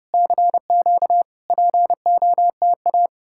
Morse code, heard as a keyed tone, about 34 wpm, 700 Hz. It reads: CQ POTA